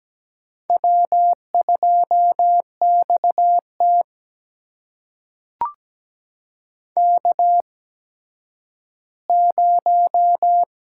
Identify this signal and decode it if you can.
Morse code, heard as a keyed tone, about 17 wpm, spaced as for 10 wpm, 700 Hz. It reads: W2XT K0
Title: USA Call Signs - 4 Characters - 8x Character Spacing - 17wpm